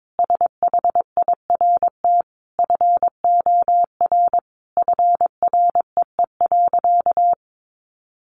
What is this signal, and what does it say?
Morse code, heard as a keyed tone, about 22 wpm, 700 Hz. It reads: SHIRT FOR FREE.
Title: Additional Easy Phrases 22wpm